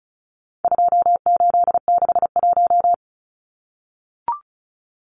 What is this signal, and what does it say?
Morse code, heard as a keyed tone, about 34 wpm, 700 Hz. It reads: 2861
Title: Numbers - 4-Digit 35wpm